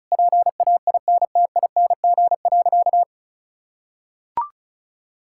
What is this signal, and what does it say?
Morse code, heard as a keyed tone, about 36 wpm, 700 Hz. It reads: PAINTING.
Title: Sentences from Top 1000 Words - Review Part 6 - 35wpm